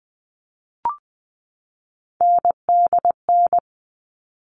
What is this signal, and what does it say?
Morse code, heard as a keyed tone, about 20 wpm, 700 Hz. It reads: NDN
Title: All Cut Numbers - 3-Digit 20wpm